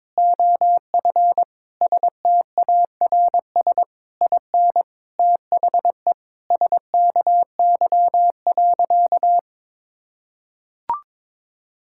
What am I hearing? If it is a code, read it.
Morse code, heard as a keyed tone, about 22 wpm, 700 Hz. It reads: OF STARS IN THE SKY.
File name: practice / 22wpm / Sentences with 4-Word Phrases - 22wpm